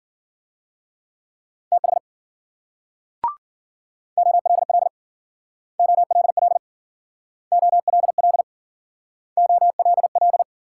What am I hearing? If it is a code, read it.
Morse code, heard as a keyed tone, about 53 wpm, 700 Hz. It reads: TH OLL OLL OLL OLL